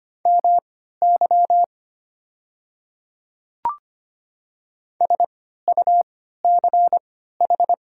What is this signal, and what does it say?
Morse code, heard as a keyed tone, about 25 wpm, spaced as for 14 wpm, 690 Hz. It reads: MY SUCH
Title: 3x Character Spacing - Top 200 Words - 25wpm